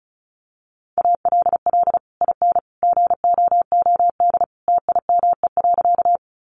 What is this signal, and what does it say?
Morse code, heard as a keyed tone, about 34 wpm, 700 Hz. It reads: ALL IN GOOD TIME.
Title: Sentences from Top 200 Words 35wpm